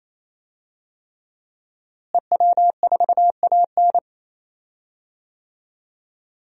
Morse code, heard as a keyed tone, about 28 wpm, 700 Hz. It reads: EW4AN
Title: DXCC-100 Call Signs - 5 Characters - Repeated 3x - 28wpm